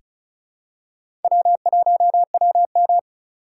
Morse code, heard as a keyed tone, about 34 wpm, 700 Hz. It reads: W1WM
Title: USA Call Signs - 4 Characters - 3x Character Spacing - 35wpm